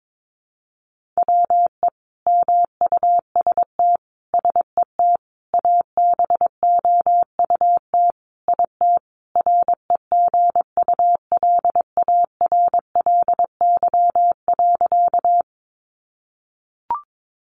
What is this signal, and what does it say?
Morse code, heard as a keyed tone, about 22 wpm, 700 Hz. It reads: WE MUST SET ABOUT IT REGULARLY.
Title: Sentences from Top 1400 Words 22wpm